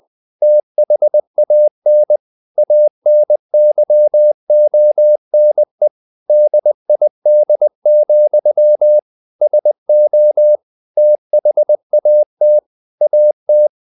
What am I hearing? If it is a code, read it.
Morse code, heard as a keyed tone, about 20 wpm, 600 Hz. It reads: THAN ANYONE DID, SO THAT AT
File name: the old man and the sea / the old man and the sea part 1 20wpm